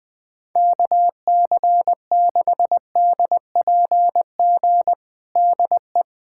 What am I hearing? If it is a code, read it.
Morse code, heard as a keyed tone, about 21 wpm, 700 Hz. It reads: KC6DPG DE
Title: USA Callsign DE Callsign - 6 Characters - 20wpm